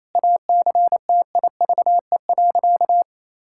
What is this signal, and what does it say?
Morse code, heard as a keyed tone, about 28 wpm, 700 Hz. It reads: ACTIVE.
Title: Sentences from Top 1000 Words - Review Part 6 - 28wpm